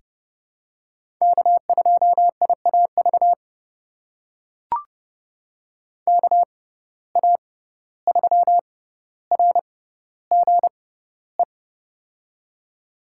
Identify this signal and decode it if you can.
Morse code, heard as a keyed tone, about 30 wpm, spaced as for 15 wpm, 700 Hz. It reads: K2IAV KA3RGE
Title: USA Call Signs - 6x Character Spacing - 30wpm